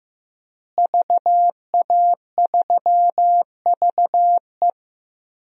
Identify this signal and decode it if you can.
Morse code, heard as a keyed tone, about 15 wpm, 700 Hz. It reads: VA3VE
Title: DXCC-100 Call Signs - 5 Characters - Repeated 3x - 15wpm